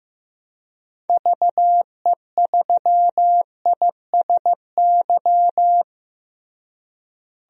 Morse code, heard as a keyed tone, about 15 wpm, 700 Hz. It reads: VE3ISY
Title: DXCC-100 Call Signs - 6 Characters - Repeated 2x - 15wpm